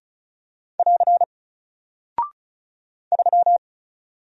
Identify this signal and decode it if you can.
Morse code, heard as a keyed tone, about 36 wpm, 700 Hz. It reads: <AR> 3
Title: Single Character (Letter:Number:Punctuation:Prosign) Alt 35wpm